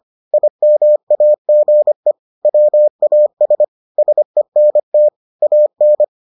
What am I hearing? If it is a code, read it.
Morse code, heard as a keyed tone, about 25 wpm, 600 Hz. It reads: IMAGE WAS SENT AN